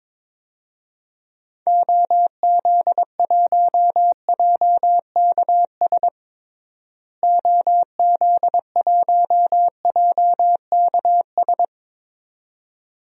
Morse code, heard as a keyed tone, about 22 wpm, 700 Hz. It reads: OZ1JKS OZ1JKS
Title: DXCC-100 Call Signs - 6 Characters - Repeated 2x - 22wpm